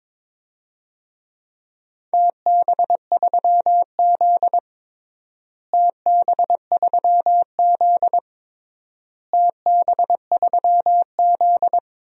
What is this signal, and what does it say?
Morse code, heard as a keyed tone, about 22 wpm, 700 Hz. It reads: TB3Z TB3Z TB3Z
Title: DXCC-100 Call Signs - 4 Characters - Repeated 3x - 22wpm